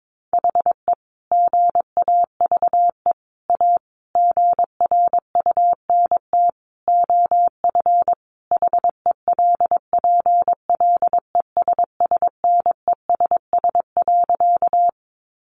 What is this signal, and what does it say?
Morse code, heard as a keyed tone, about 22 wpm, 700 Hz. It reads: HE GAVE A GRUNT OF HELPLESSNESS.
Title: A Sound Of Thunder 22wpm